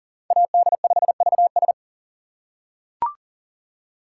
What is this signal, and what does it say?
Morse code, heard as a keyed tone, about 40 wpm, 700 Hz. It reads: AD5VS
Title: USA Call Signs - 5 Characters - 2x Character Spacing - 40wpm